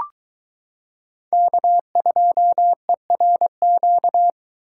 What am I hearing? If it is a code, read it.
Morse code, heard as a keyed tone, about 23 wpm, 700 Hz. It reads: K2ERQ